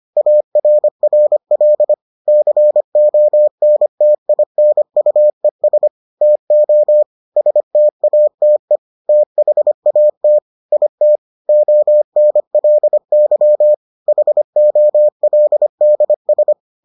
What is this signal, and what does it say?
Morse code, heard as a keyed tone, about 25 wpm, 600 Hz. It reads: ARRL CONTINUES TO STATE THAT IT ONLY HOLDS